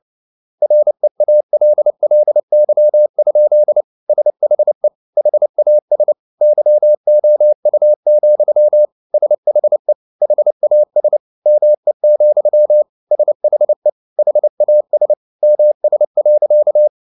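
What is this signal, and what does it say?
Morse code, heard as a keyed tone, about 29 wpm, 600 Hz. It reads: REALLY? SHE HAS YOU, SHE HAS ME, SHE HAS MS.
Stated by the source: Morse code at 29 wpm, sending Ken Follett's novel Never in English, from CW med Gnister.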